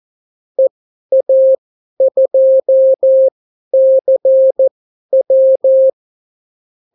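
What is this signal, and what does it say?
Morse code, heard as a keyed tone, about 14 wpm, spaced as for 11 wpm, 530 Hz. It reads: EA2CW
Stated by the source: Example Spanish CW QSO from a Morse learning book, 14 wpm characters with 8 wpm spacing.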